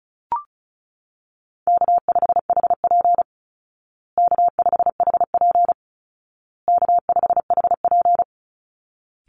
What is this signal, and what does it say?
Morse code, heard as a keyed tone, about 34 wpm, 700 Hz. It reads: K5HP K5HP K5HP
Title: Easy Call Signs Repeated 3x - 35wpm